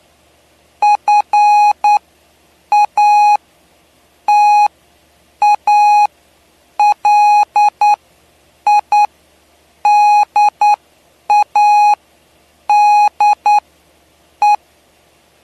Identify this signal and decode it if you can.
Morse code, heard as a keyed tone, about 9 wpm, spaced as for 7 wpm, 830 Hz. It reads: FATALIDADE
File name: cw / py2lcd / 07-PY2LCD-Li��o 7